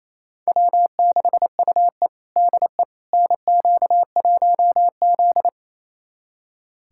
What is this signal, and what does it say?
Morse code, heard as a keyed tone, about 28 wpm, 700 Hz. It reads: W6UE DE NQ1Z
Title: USA Callsign DE Callsign - 4 Characters - 28wpm